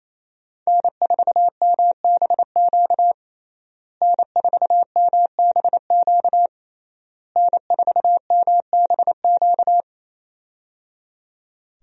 Morse code, heard as a keyed tone, about 28 wpm, 700 Hz. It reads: N4MBQ N4MBQ N4MBQ
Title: USA Call Signs - 5 Characters - Repeated 3x - 28wpm